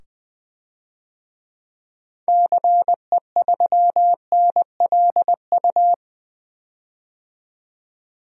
Morse code, heard as a keyed tone, about 20 wpm, 700 Hz. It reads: CE3NLU
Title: DXCC-100 Call Signs - 6 Characters - 20wpm